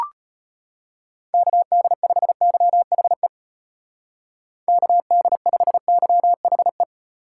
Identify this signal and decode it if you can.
Morse code, heard as a keyed tone, about 30 wpm, 700 Hz. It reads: KD5YHE KD5YHE
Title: Speed Racing - USA Call Signs 25wpm